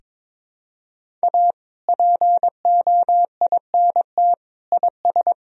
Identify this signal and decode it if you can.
Morse code, heard as a keyed tone, about 22 wpm, 700 Hz. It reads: A POINT IS
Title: Sentences from Top 1000 Words - Review Part 1 - 22wpm